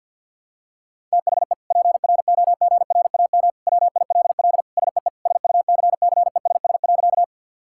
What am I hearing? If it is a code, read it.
Morse code, heard as a keyed tone, about 52 wpm, 690 Hz. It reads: THE PROGRAM WILL SEE SUCCESS.
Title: Sentences from Top 1000 Words - Review Part 5 - 50wpm